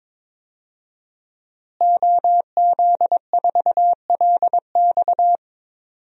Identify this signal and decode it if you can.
Morse code, heard as a keyed tone, about 22 wpm, 700 Hz. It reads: OZ4LX
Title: DXCC-100 Call Signs - 5 Characters - Repeated 2x - 22wpm